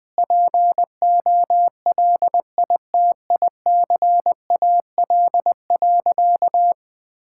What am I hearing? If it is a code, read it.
Morse code, heard as a keyed tone, about 21 wpm, 700 Hz. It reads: POLITICAL.
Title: Sentences from Top 1600 Words 20wpm